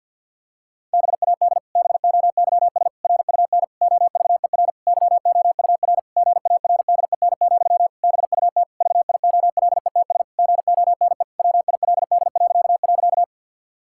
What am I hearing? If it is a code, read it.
Morse code, heard as a keyed tone, about 51 wpm, 700 Hz. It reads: BAD BOYS RUN OVER YOUR GARDEN, BUT VIOLETS GONE WILD..
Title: Pangram Challenge v2 - 50wpm